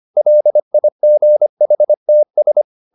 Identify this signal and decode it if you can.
Morse code, heard as a keyed tone, about 25 wpm, 600 Hz. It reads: LIGHTS